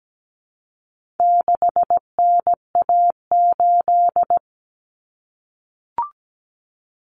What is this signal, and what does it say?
Morse code, heard as a keyed tone, about 17 wpm, 700 Hz. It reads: 6NA8
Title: Common Cut Numbers - 4-Digit 17wpm